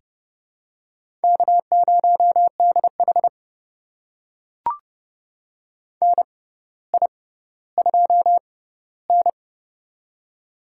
Morse code, heard as a keyed tone, about 30 wpm, spaced as for 15 wpm, 700 Hz. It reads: K0DH NI2N
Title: USA Call Signs - 4 Characters - 6x Character Spacing - 30wpm